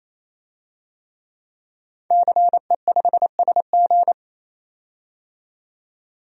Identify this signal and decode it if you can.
Morse code, heard as a keyed tone, about 27 wpm, 700 Hz. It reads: CE5SG
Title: DXCC-100 Call Signs - 5 Characters - 28wpm